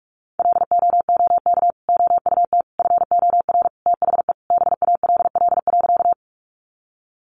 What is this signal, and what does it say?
Morse code, heard as a keyed tone, about 44 wpm, 700 Hz. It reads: LOOK OUT FOR THE BALL.
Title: Sentences from Top 900 Words 45wpm